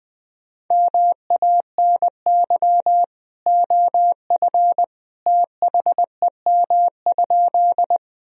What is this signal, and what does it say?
Morse code, heard as a keyed tone, about 20 wpm, 700 Hz. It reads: MANY OF THEM?
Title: Sentences from Top 1000 Words - Review Part 1 - 20wpm